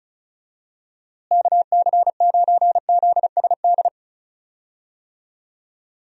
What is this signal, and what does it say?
Morse code, heard as a keyed tone, about 36 wpm, 700 Hz. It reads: KC9ZSD
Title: USA Call Signs - 6 Characters - Repeated 3x - 35wpm